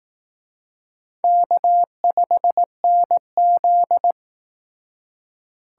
Morse code, heard as a keyed tone, about 18 wpm, 700 Hz. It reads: K5NZ
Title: Speed Racing - USA 1x2 Call Signs 15wpm